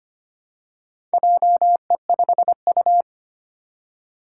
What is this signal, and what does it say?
Morse code, heard as a keyed tone, about 25 wpm, 700 Hz. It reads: JE5U
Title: Four Challenging Letter-Number Combinations 25wpm